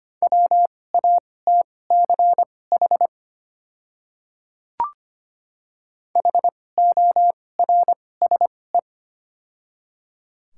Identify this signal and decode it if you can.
Morse code, heard as a keyed tone, about 25 wpm, spaced as for 18 wpm, 690 Hz. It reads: WATCH HORSE
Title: 2x Character Spacing - Top 300 Words - 25wpm